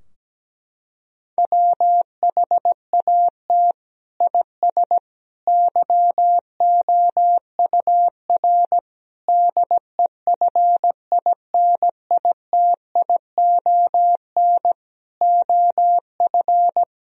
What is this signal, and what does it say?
Morse code, heard as a keyed tone, about 17 wpm, 700 Hz. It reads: WHAT IS YOUR DEFINITION OF
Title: Sentences from Top 1000 Words - Review Part 4 - 17wpm